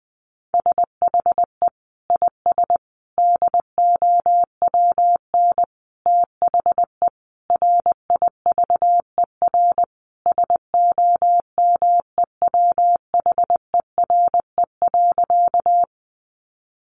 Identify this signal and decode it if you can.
Morse code, heard as a keyed tone, about 20 wpm, 700 Hz. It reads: SHE IS DOWN THE RIVER SOMEWHERE.
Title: Sentences from Top 1400 Words 20wpm